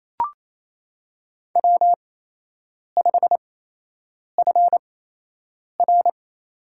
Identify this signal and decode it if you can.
Morse code, heard as a keyed tone, about 28 wpm, spaced as for 8 wpm, 700 Hz. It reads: W5FR
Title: USA Call Signs - 8x Character Spacing - 28wpm